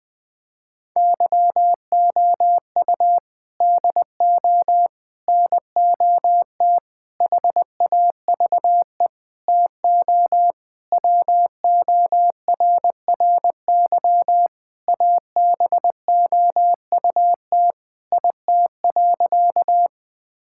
Morse code, heard as a keyed tone, about 21 wpm, 700 Hz. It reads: YOU DO NOT HAVE TO WORRY ABOUT IT.
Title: Sentences with 5-Word Phrases - 20wpm